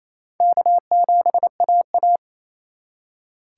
Morse code, heard as a keyed tone, about 34 wpm, 700 Hz. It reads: K7AA
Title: Speed Racing - USA 1x2 Call Signs 28wpm